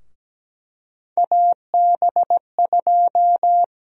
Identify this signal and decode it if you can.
Morse code, heard as a keyed tone, about 17 wpm, 700 Hz. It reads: AB2
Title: USA Call Sign Prefixes - 17wpm